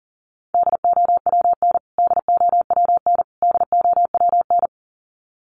Morse code, heard as a keyed tone, about 40 wpm, 700 Hz. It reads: DOWN DOWN DOWN
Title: Top 200 Words Repeated 40wpm